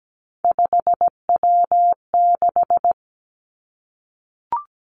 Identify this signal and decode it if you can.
Morse code, heard as a keyed tone, about 17 wpm, 700 Hz. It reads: 5W6
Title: Three Challenging Letter-Number Combinations 17wpm